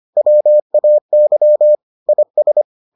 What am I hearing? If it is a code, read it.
Morse code, heard as a keyed tone, about 25 wpm, 600 Hz. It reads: WAY IS